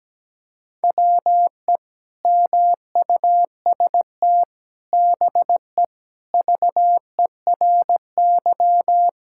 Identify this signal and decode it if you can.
Morse code, heard as a keyed tone, about 17 wpm, 700 Hz. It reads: WE MUST BE VERY
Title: Sentences from Top 1600 Words 17wpm